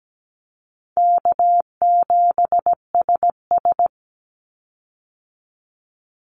Morse code, heard as a keyed tone, about 17 wpm, 700 Hz. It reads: K7SS
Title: Easy Call Signs - 17wpm